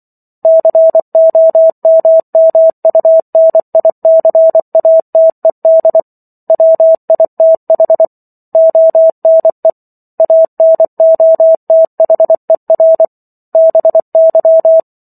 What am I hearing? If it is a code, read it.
Morse code, heard as a keyed tone, about 24 wpm, 650 Hz. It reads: COMMUNICATED WITH ONE ANOTHER BY